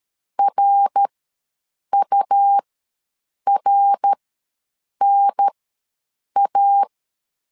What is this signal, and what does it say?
Morse code, heard as a keyed tone, about 13 wpm, spaced as for 7 wpm, 800 Hz. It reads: RURNA
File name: LEZIONI / 3 MWRO / 3 MWR ws 12wpm 1809